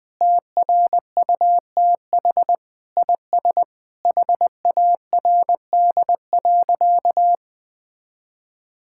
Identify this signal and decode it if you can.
Morse code, heard as a keyed tone, about 20 wpm, 700 Hz. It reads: TRUTH IS HARD.
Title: Sentences with 6-Word Phrases - 20wpm